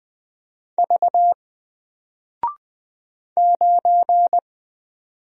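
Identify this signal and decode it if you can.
Morse code, heard as a keyed tone, about 20 wpm, 700 Hz. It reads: V 9
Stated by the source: Single letter and number Morse drill at 20 wpm, from N4LCD.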